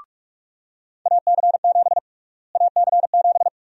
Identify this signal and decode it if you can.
Morse code, heard as a keyed tone, about 44 wpm, 700 Hz. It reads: AC7 AC7
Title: USA Call Sign Prefixes - Repeated 2x - 45wpm